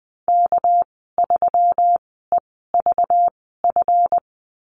Morse code, heard as a keyed tone, about 20 wpm, spaced as for 14 wpm, 700 Hz. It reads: K3EVF
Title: 213 Lesson 34g - Number 8 - Call Signs 20wpm